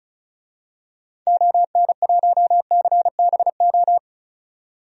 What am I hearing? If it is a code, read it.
Morse code, heard as a keyed tone, about 36 wpm, 700 Hz. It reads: ON1CBO
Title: DXCC-100 Call Signs - 6 Characters - 35wpm